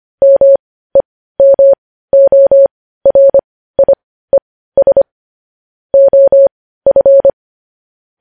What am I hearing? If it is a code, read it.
Morse code, heard as a keyed tone, about 25 wpm, spaced as for 15 wpm, 550 Hz. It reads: MEMORIES OF
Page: news headlines in morse code at 15 wpm